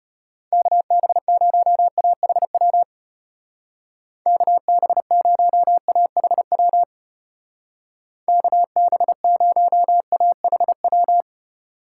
Morse code, heard as a keyed tone, about 32 wpm, 700 Hz. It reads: KB0AHW KB0AHW KB0AHW